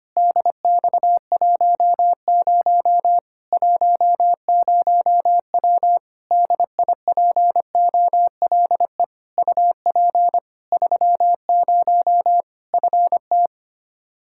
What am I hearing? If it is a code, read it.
Morse code, heard as a keyed tone, about 25 wpm, 700 Hz. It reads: DX10 10W DIPOLE UP 30 FT
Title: Practice QSO - Exchange 2 - Info - 25wpm